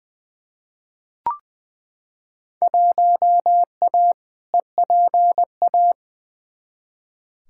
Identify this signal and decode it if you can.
Morse code, heard as a keyed tone, about 20 wpm, 700 Hz. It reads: 1A EPA
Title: Field Day Exchanges - 20wpm